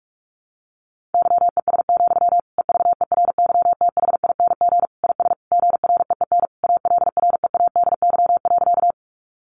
Morse code, heard as a keyed tone, about 44 wpm, 700 Hz. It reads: YES, EVERYTHING IS GREEN ALREADY.
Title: Sentences from Top 700 Words 45wpm